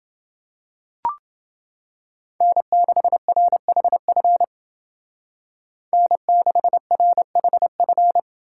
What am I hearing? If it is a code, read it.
Morse code, heard as a keyed tone, about 29 wpm, 700 Hz. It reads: N6RHF N6RHF